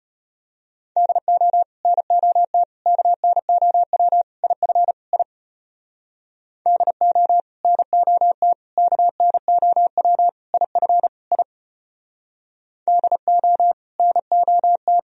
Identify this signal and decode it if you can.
Morse code, heard as a keyed tone, about 33 wpm, 700 Hz. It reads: DO NOT KNOW IF I DO NOT KNOW IF I DO NOT